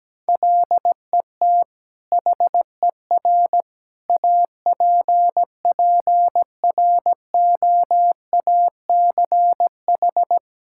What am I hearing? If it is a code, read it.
Morse code, heard as a keyed tone, about 17 wpm, 700 Hz. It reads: LET HER APPROACH
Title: Sentences from Top 1000 Words - Review Part 6 - 17wpm